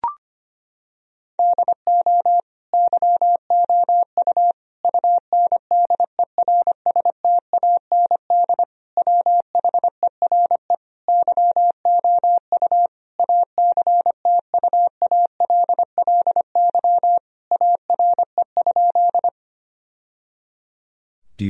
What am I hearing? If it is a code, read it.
Morse code, heard as a keyed tone, about 25 wpm, 700 Hz. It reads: DO YOU UNDERSTAND WHERE YOU ACTUALLY ARE?
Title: Sentences from Top 1000 Words - Review Part 4 - 25wpm